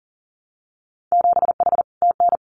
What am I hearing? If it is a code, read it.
Morse code, heard as a keyed tone, about 40 wpm, 700 Hz. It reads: 7H TN
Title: Winter Field Day Exchanges Encore 40wpm